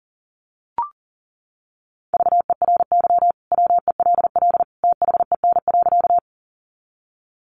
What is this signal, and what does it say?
Morse code, heard as a keyed tone, about 40 wpm, 700 Hz. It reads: VERY WELL THEN.
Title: Sentences from Top 200 Words 40wpm